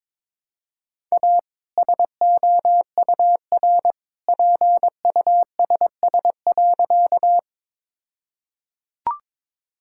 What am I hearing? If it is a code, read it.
Morse code, heard as a keyed tone, about 22 wpm, 700 Hz. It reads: A SOUR PUSS.
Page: morse-code-ninja-practice-22wpm